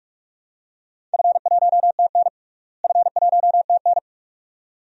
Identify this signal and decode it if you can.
Morse code, heard as a keyed tone, about 47 wpm, 700 Hz. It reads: F1TN F1TN
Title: DXCC-100 Call Signs - 4 Characters - Repeated 2x - 45wpm